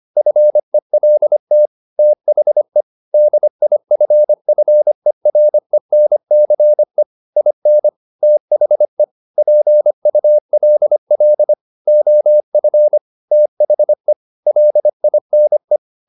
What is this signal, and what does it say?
Morse code, heard as a keyed tone, about 25 wpm, 600 Hz. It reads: FELT THE DIFFERENCE IN THE PULL OF THE LINE
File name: the old man and the sea part 1 25wpm